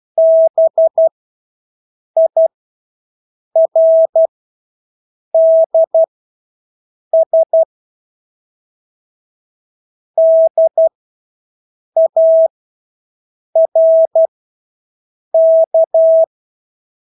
Morse code, heard as a keyed tone, about 12 wpm, spaced as for 6 wpm, 650 Hz. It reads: BIRDS DARK
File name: Engelske ord 12-6 wpm